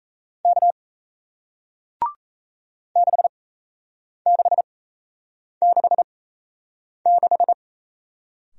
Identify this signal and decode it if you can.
Morse code, heard as a keyed tone, about 37 wpm, spaced as for 8 wpm, 700 Hz. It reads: K 6666